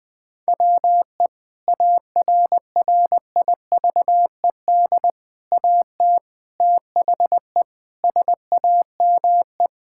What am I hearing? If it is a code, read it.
Morse code, heard as a keyed tone, about 20 wpm, 700 Hz. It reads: WE ARRIVED AT THE SAME
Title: Sentences with 3-Word Phrases - Spoken Beginning - 20wpm